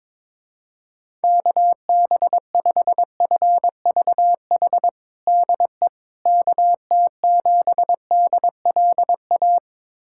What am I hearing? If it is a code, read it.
Morse code, heard as a keyed tone, about 22 wpm, 700 Hz. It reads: KB5FVH DE KT7DLA
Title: USA Callsign DE Callsign 22wpm